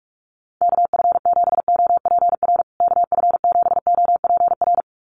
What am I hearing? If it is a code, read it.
Morse code, heard as a keyed tone, about 44 wpm, 700 Hz. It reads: KF7OPR KF7OPR
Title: USA Call Signs Repeated 2x 45wpm